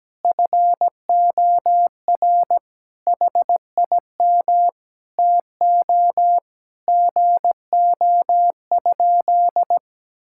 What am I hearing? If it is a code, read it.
Morse code, heard as a keyed tone, about 17 wpm, 700 Hz. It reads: FOR HIM TO GO?
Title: Sentences from Top 1600 Words 17wpm